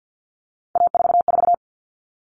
Morse code, heard as a keyed tone, about 50 wpm, 700 Hz. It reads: A44
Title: All Cut Numbers - 3-Digit 50wpm